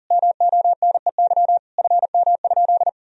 Morse code, heard as a keyed tone, about 40 wpm, 700 Hz. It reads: MONEY FM?
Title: Sentences from Top 400 Words and Abbreviations 40wpm